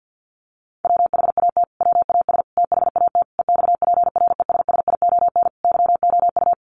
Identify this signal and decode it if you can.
Morse code, heard as a keyed tone, about 50 wpm, 700 Hz. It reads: WHAT WAS THAT EXPRESSION YOU
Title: Sentences from Top 1200 Words 50wpm